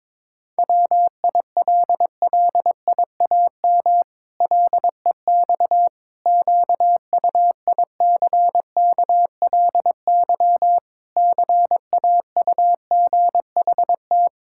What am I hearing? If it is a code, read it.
Morse code, heard as a keyed tone, about 22 wpm, 700 Hz. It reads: WILLIAM LEX QUICKLY CAUGHT
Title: Pangram Challenge v2 - 22wpm